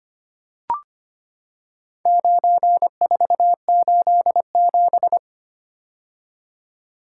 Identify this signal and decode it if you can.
Morse code, heard as a keyed tone, about 25 wpm, 700 Hz. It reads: 9487
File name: practice / 25wpm / Four Letter-Number Combinations - 25wpm